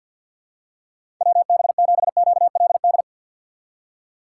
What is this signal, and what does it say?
Morse code, heard as a keyed tone, about 53 wpm, 700 Hz. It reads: WB7QLD
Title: USA Call Signs - 6 Characters - 3x Character Spacing - 50wpm